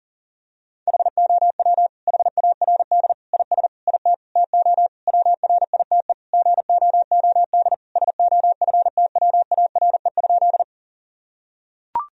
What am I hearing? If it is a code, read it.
Morse code, heard as a keyed tone, about 40 wpm, 700 Hz. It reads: HOW HARD IS IT TO WRITE GOOD SOFTWARE?
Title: Sentences from Top 1000 Words - Review Part 3 - 40wpm